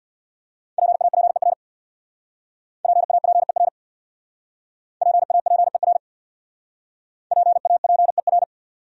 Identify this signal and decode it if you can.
Morse code, heard as a keyed tone, about 64 wpm, 700 Hz. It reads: PAPER PAPER PAPER PAPER